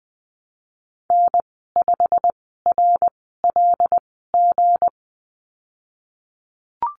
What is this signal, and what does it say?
Morse code, heard as a keyed tone, about 20 wpm, spaced as for 15 wpm, 700 Hz. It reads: N5RLG
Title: 173 Lesson 28g - Number 9 - Call Signs 20wpm